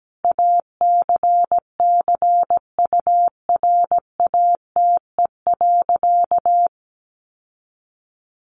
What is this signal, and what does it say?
Morse code, heard as a keyed tone, about 17 wpm, 700 Hz. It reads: ACCURATE.
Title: Sentences from Top 1200 Words 17wpm